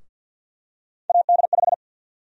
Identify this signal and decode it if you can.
Morse code, heard as a keyed tone, about 48 wpm, 700 Hz. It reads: AD5
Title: USA Call Sign Prefixes - Repeated 2x - 50wpm